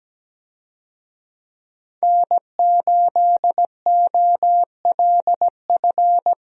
Morse code, heard as a keyed tone, about 17 wpm, 700 Hz. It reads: N8OLF
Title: USA Call Signs - 5 Characters - 17wpm